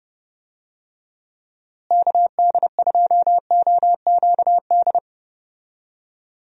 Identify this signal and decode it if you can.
Morse code, heard as a keyed tone, about 30 wpm, 700 Hz. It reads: KD2OQD
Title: USA Call Signs - 6 Characters - Repeated 3x - 30wpm